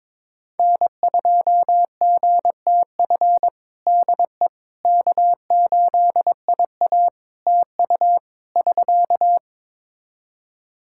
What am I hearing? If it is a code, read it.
Morse code, heard as a keyed tone, about 22 wpm, 700 Hz. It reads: N2GTF DE K8IA TU <SK>
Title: Practice QSO - Exchange 4 - Ending - 22wpm